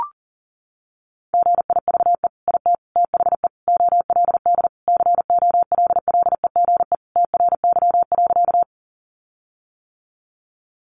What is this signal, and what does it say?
Morse code, heard as a keyed tone, about 40 wpm, 700 Hz. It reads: GIVE IT THE OLD COLLEGE TRY.
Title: Sentences from Top 1300 Words 40wpm